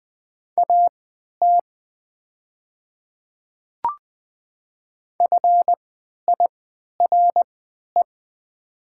Morse code, heard as a keyed tone, about 20 wpm, spaced as for 11 wpm, 700 Hz. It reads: AT FIRE
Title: 3x Character Spacing - Top 500 Words - 20wpm